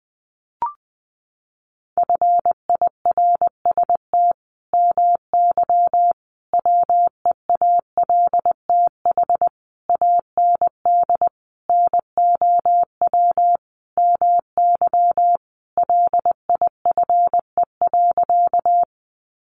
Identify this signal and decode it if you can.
Morse code, heard as a keyed tone, about 20 wpm, 700 Hz. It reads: FIRST MY WEALTH AND NOW MY LIFE.
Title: Sentences from Top 1200 Words 20wpm